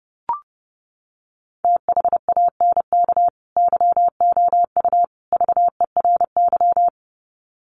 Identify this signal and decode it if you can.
Morse code, heard as a keyed tone, about 30 wpm, 700 Hz. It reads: THANK YOU VERY
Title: Sets of 3 Words 30wpm